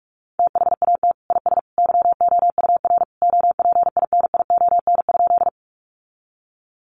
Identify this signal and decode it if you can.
Morse code, heard as a keyed tone, about 44 wpm, 700 Hz. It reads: THAT IS YOUR OPINION?
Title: Sentences from Top 1200 Words 45wpm